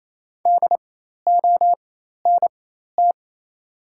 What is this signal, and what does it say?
Morse code, heard as a keyed tone, about 27 wpm, spaced as for 13 wpm, 700 Hz. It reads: DONT
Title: 4x Character Spacing - Top 300 Words - 28wpm